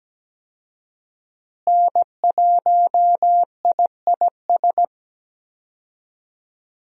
Morse code, heard as a keyed tone, about 17 wpm, 700 Hz. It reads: N1IIS
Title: USA Call Signs - 5 Characters - Repeated 3x - 17wpm